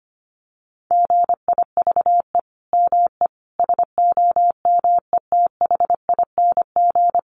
Morse code, heard as a keyed tone, about 25 wpm, 700 Hz. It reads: GIVE ME SOMETHING
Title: Sentences from Top 1500 Words 25wpm